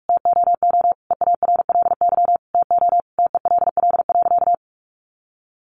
Morse code, heard as a keyed tone, about 44 wpm, 700 Hz. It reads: TOO EARLY TO TELL.
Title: A Sound Of Thunder 45wpm